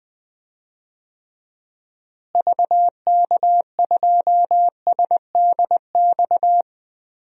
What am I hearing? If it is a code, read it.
Morse code, heard as a keyed tone, about 21 wpm, 700 Hz. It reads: VK2SDX
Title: DXCC-100 Call Signs - 6 Characters - Repeated 3x - 20wpm